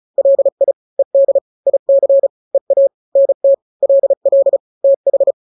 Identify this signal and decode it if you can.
Morse code, heard as a keyed tone, about 34 wpm, 550 Hz. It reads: LI ED IC EA NT LL TH
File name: n-grams / 2-grams 35 30 WPM